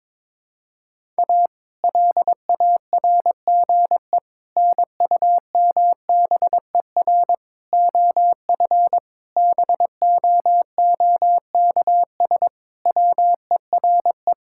Morse code, heard as a keyed tone, about 22 wpm, 700 Hz. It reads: A LARGE NUMBER OF BOOKS WERE
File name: practice / 22wpm / Sentences with 4-Word Phrases - 22wpm